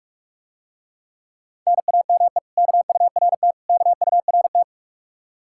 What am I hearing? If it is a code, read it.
Morse code, heard as a keyed tone, about 50 wpm, 700 Hz. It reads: NAME KURT KURT